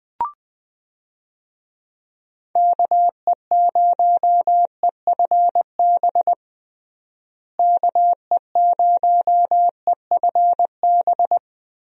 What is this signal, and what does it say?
Morse code, heard as a keyed tone, about 21 wpm, 700 Hz. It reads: KE0EFB KE0EFB
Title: USA Call Signs - 6 Characters - Repeated 3x - 20wpm